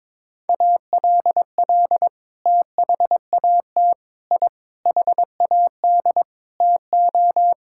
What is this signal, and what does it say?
Morse code, heard as a keyed tone, about 22 wpm, 700 Hz. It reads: ALL THAT I HAD TO
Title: Sentences from Top 1000 Words - Review Part 6 - 22wpm